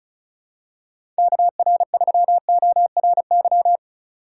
Morse code, heard as a keyed tone, about 36 wpm, 700 Hz. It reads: KR3ORY
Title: USA Call Signs - 6 Characters - 3x Character Spacing - 35wpm